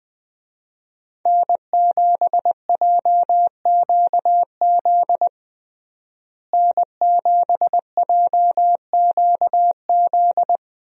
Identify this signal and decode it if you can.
Morse code, heard as a keyed tone, about 20 wpm, 700 Hz. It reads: N7JQZ N7JQZ
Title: USA Call Signs - 5 Characters - Repeated 3x - 20wpm